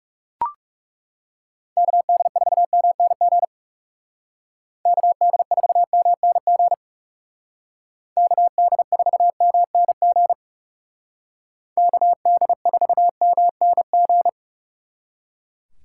Morse code, heard as a keyed tone, about 37 wpm, 700 Hz. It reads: KD4MNG KD4MNG KD4MNG KD4MNG
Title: Speed Racing - USA 2x3 Call Signs 30wpm